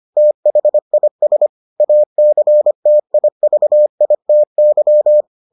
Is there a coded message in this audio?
Yes: Morse code from 0.2 to 5.2 s, heard as a keyed tone, about 25 wpm, 600 Hz. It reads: THIS ACTIVITY